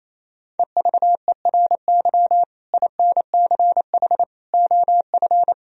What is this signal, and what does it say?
Morse code, heard as a keyed tone, about 28 wpm, 700 Hz. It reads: EVERY INCH OF
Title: Sentences from Top 1000 Words - Review Part 3 - 28wpm